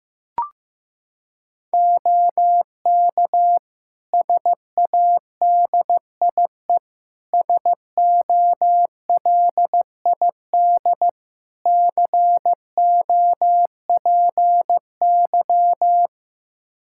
Morse code, heard as a keyed tone, about 16 wpm, 700 Hz. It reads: OK SADIE SOLID COPY
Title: Practice QSO - Exchange 3 - More Info - 15wpm